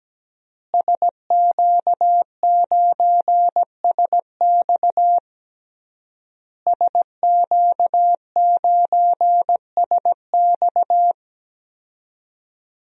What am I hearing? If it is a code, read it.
Morse code, heard as a keyed tone, about 17 wpm, 700 Hz. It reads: SQ9SX SQ9SX
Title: DXCC-100 Call Signs - 5 Characters - Repeated 3x - 17wpm